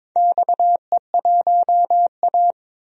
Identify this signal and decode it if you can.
Morse code, heard as a keyed tone, about 22 wpm, 700 Hz. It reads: XE1A
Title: DXCC-100 Call Signs - 4 Characters - 22wpm